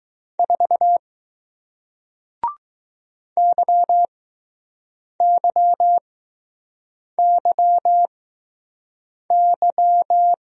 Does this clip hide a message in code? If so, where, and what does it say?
Yes: Morse code from 0.4 to 10.3 s, heard as a keyed tone, about 20 wpm, spaced as for 6 wpm, 700 Hz. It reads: 4 YYYY